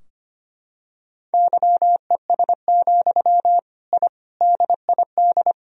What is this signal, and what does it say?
Morse code, heard as a keyed tone, about 25 wpm, 700 Hz. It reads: YES, I DID
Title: Sentences from Top 1000 Words - Review Part 2 - 25wpm